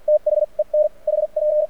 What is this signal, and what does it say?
Morse code, heard as a keyed tone, about 48 wpm, spaced as for 38 wpm, 600 Hz, with radio noise. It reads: THEN HP
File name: archived / CRSnet